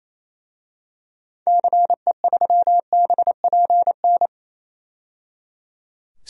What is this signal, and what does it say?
Morse code, heard as a keyed tone, about 28 wpm, 700 Hz. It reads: CE3BPN
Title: DXCC-100 Call Signs - 6 Characters - 28wpm